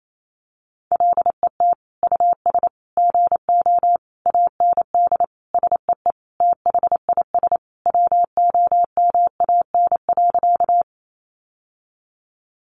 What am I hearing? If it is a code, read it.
Morse code, heard as a keyed tone, about 28 wpm, 700 Hz. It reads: LET US GO AND SEE THIS WOMAN.
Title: Sentences from Top 1100 Words 28wpm